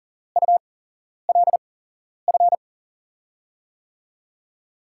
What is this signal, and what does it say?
Morse code, heard as a keyed tone, about 40 wpm, spaced as for 11 wpm, 700 Hz. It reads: ULF
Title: USA Call Signs - 6 Characters - 8x Character Spacing - 40wpm